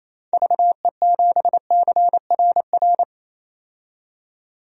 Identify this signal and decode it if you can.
Morse code, heard as a keyed tone, about 28 wpm, 700 Hz. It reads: VE7CRR
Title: DXCC-100 Call Signs - 6 Characters - 28wpm